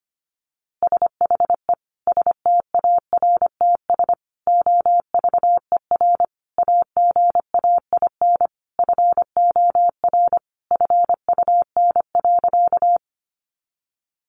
Morse code, heard as a keyed tone, about 25 wpm, 700 Hz. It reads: SHE STARTS OVER AGAIN FOR FUN.
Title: Easy Phrases 25wpm